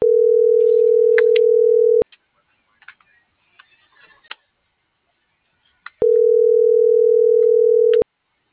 Ringback tone (North America), 440 + 480 Hz.